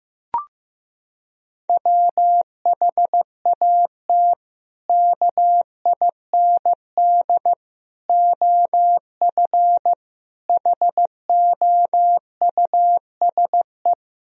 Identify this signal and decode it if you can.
Morse code, heard as a keyed tone, about 15 wpm, 700 Hz. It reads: WHAT KIND OF HOUSE
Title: Sentences from Top 1000 Words - Review Part 1 - 15wpm